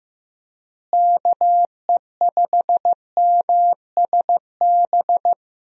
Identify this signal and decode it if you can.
Morse code, heard as a keyed tone, about 15 wpm, 700 Hz. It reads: KE5MSB
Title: USA Call Signs - 6 Characters - Repeated 3x - 15wpm